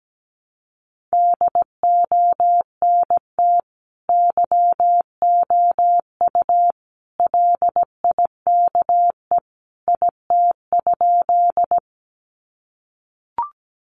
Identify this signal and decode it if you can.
Morse code, heard as a keyed tone, about 17 wpm, 700 Hz. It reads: DONT YOU LIKE IT?
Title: Sentences from Top 300 Words 17wpm